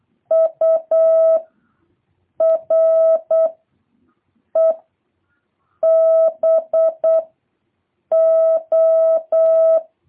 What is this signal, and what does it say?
Morse code, heard as a keyed tone, about 8 wpm, 650 Hz. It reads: U R E B O